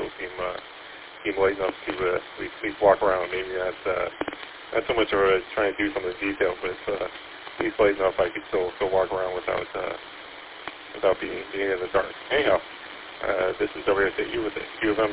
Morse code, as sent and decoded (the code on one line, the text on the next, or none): none